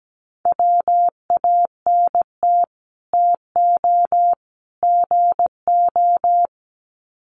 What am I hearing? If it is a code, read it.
Morse code, heard as a keyed tone, about 17 wpm, 700 Hz. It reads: WANT TO GO
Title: Sets of 3 Words 17wpm